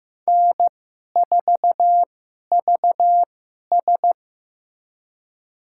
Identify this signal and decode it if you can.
Morse code, heard as a keyed tone, about 15 wpm, spaced as for 11 wpm, 700 Hz. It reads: N4VS
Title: USA Call Signs - 4 Characters - 2x Character Spacing - 15wpm